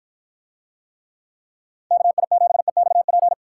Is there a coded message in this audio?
Yes: Morse code from 1.9 to 3.3 s, heard as a keyed tone, about 53 wpm, 700 Hz. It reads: KI7EXP